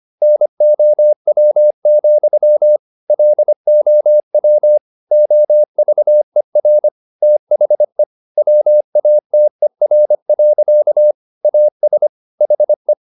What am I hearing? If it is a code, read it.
Morse code, heard as a keyed tone, about 25 wpm, 600 Hz. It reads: NOW, LOW OVER THE WATER. AS HE
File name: the old man and the sea part 1 25wpm